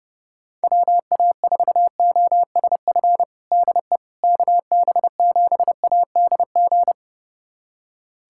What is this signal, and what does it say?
Morse code, heard as a keyed tone, about 32 wpm, 700 Hz. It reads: WA4OSF DE KB7ADG
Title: USA Callsign DE Callsign - 6 Characters - 30wpm